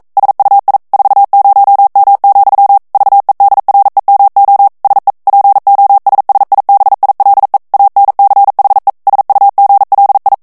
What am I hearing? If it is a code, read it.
Morse code, heard as a keyed tone, about 42 wpm, 800 Hz. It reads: SUI 40M, VEDREMO SE POSSIBILE ANCHE SUGLI
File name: morse mp3 / ITALIEN texte clair 1 vitesse 40 mots minute